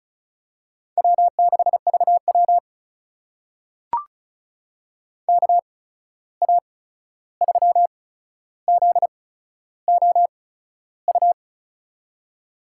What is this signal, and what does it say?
Morse code, heard as a keyed tone, about 36 wpm, spaced as for 13 wpm, 700 Hz. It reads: W6VW KA3ZOU
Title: USA Call Signs - 8x Character Spacing - 35wpm